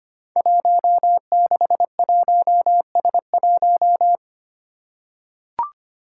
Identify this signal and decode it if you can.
Morse code, heard as a keyed tone, about 25 wpm, 700 Hz. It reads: 161S1
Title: Five Challenging Letter-Number Combinations 25wpm